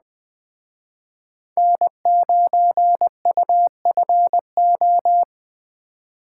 Morse code, heard as a keyed tone, about 20 wpm, 700 Hz. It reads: N9UFO
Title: Speed Racing - USA Call Signs 17wpm